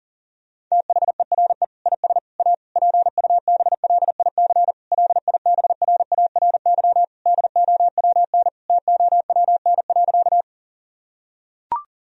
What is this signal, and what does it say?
Morse code, heard as a keyed tone, about 40 wpm, 700 Hz. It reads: THERE IS A PUBLIC LIBRARY DOWN TOWN.
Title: Sentences from Top 1000 Words - Review Part 4 - 40wpm